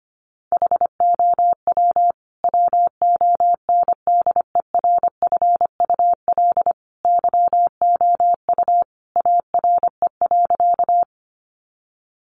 Morse code, heard as a keyed tone, about 25 wpm, 700 Hz. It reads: HOW WONDERFUL YOU ARE.
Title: Sentences from Top 1100 Words 25wpm